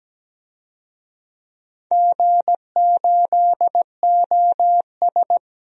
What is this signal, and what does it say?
Morse code, heard as a keyed tone, about 17 wpm, 700 Hz. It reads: G8OS
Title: DXCC-100 Call Signs - 4 Characters - Repeated 2x - 17wpm